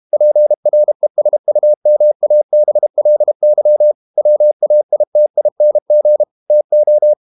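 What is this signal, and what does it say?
Morse code, heard as a keyed tone, about 32 wpm, 600 Hz. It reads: PRESUMABLY WAITING TO
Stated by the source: Morse code practice at 32 wpm from CW med Gnister, reading Ken Follett's novel Never in English.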